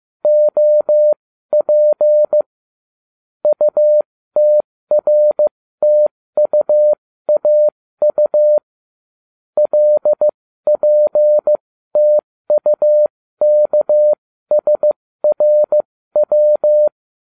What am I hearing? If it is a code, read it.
Morse code, heard as a keyed tone, about 15 wpm, spaced as for 13 wpm, 600 Hz. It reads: OP UTRTUAU LPTUKSRW